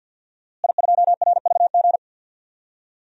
Morse code, heard as a keyed tone, about 52 wpm, 700 Hz. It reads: I2RVG
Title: DXCC-100 Call Signs - 5 Characters - Repeated 3x - 50wpm